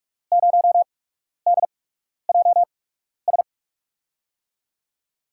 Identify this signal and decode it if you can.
Morse code, heard as a keyed tone, about 44 wpm, spaced as for 12 wpm, 700 Hz. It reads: 0DJS
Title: USA Call Signs - 8x Character Spacing - 45wpm